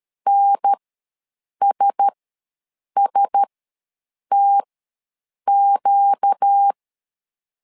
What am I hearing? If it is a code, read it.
Morse code, heard as a keyed tone, about 13 wpm, spaced as for 7 wpm, 800 Hz. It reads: NSSTQ
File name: LEZIONI / 4 Q1S ws 12wpm 1827